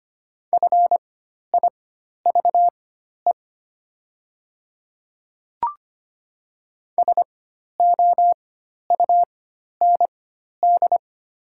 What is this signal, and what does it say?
Morse code, heard as a keyed tone, about 25 wpm, spaced as for 12 wpm, 700 Hz. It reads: FIVE SOUND
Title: 4x Character Spacing - Top 500 Words - 25wpm